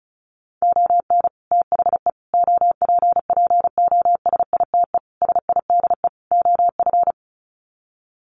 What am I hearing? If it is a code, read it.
Morse code, heard as a keyed tone, about 34 wpm, 700 Hz. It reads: ON THE OPPOSITE SIDE OF
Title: Sets of 5 Words Encore2 35wpm